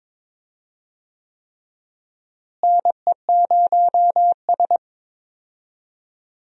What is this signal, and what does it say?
Morse code, heard as a keyed tone, about 22 wpm, 700 Hz. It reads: NE0S
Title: USA Call Signs - 4 Characters - Repeated 3x - 22wpm